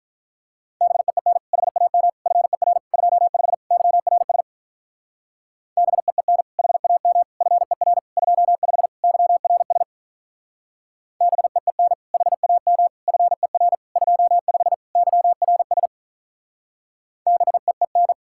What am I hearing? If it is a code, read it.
Morse code, heard as a keyed tone, about 45 wpm, 700 Hz. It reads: BEEN HAM FER 25 YRS BEEN HAM FER 25 YRS BEEN HAM FER 25 YRS BEEN